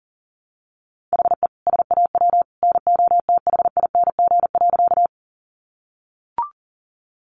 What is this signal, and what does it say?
Morse code, heard as a keyed tone, about 40 wpm, 700 Hz. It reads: HE SAW NOTHING.
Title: Sentences from Top 500 Words 40wpm